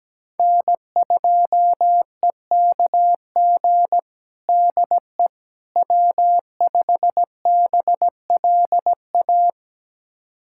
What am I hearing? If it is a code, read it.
Morse code, heard as a keyed tone, about 17 wpm, 700 Hz. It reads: N2EKG DE W5BLA
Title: USA Callsign DE Callsign - 5 Characters - 17wpm